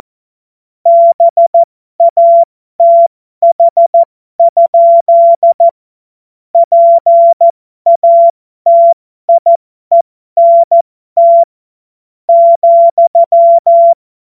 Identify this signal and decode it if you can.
Morse code, heard as a keyed tone, about 14 wpm, spaced as for 11 wpm, 680 Hz. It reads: BATH? PATIENT,
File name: Morsekurs / Witze